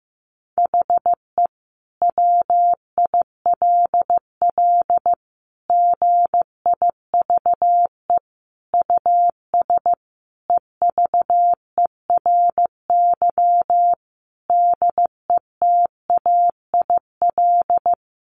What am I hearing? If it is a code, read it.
Morse code, heard as a keyed tone, about 15 wpm, 700 Hz. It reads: HE WILL GIVE US EVERY DETAIL
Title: Sentences from Top 1500 Words 15wpm